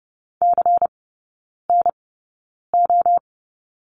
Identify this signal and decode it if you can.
Morse code, heard as a keyed tone, about 30 wpm, spaced as for 9 wpm, 690 Hz. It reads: CNO
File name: Lesson 16e - Letter c - Three Characters 30wpm